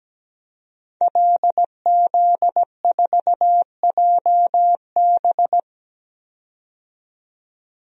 Morse code, heard as a keyed tone, about 17 wpm, 700 Hz. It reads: LZ4JB
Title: DXCC-100 Call Signs - 5 Characters - 17wpm